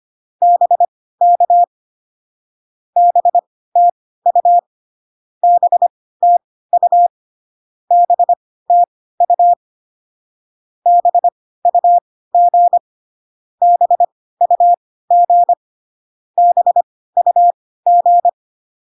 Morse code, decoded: BK BTU BTU BTU BUG BUG BUG

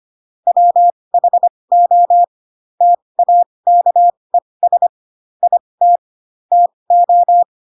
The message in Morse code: WHO TAKES IT TO

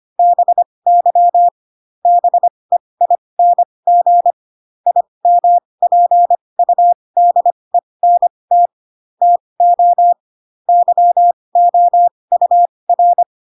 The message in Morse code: BY BEING IMPUDENT TO YOUR

